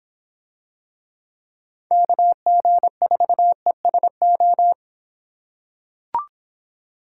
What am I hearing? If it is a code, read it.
Morse code, heard as a keyed tone, about 24 wpm, 700 Hz. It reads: KG4ESO